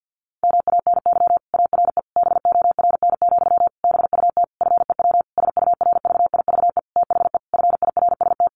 Morse code, heard as a keyed tone, about 50 wpm, 700 Hz. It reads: MANY ARE BORN, BUT FEW SURVIVE THE FIRST
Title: Sentences from Top 1300 Words 50wpm